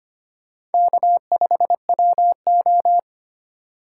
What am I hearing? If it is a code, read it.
Morse code, heard as a keyed tone, about 25 wpm, 700 Hz. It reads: K5WO
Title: USA Call Signs - 4 Characters - 2x Character Spacing - 25wpm